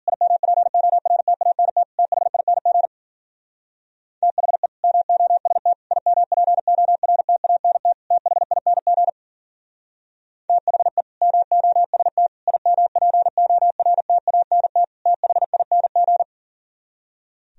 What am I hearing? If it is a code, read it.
Morse code, heard as a keyed tone, about 53 wpm, 700 Hz. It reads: IMPORTANT THING THE MOST IMPORTANT THING THE MOST IMPORTANT THING